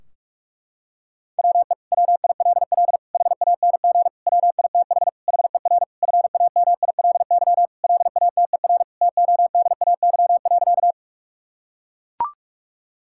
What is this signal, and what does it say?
Morse code, heard as a keyed tone, about 41 wpm, 700 Hz. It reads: WE WILL HANG WITH HER FAMILY LATER TODAY.
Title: Sentences from Top 1600 Words 45wpm